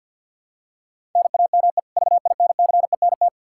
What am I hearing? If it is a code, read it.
Morse code, heard as a keyed tone, about 47 wpm, 700 Hz. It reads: NAME VINCENT
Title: Practice QSO - Exchange 1 - Beginning - 50wpm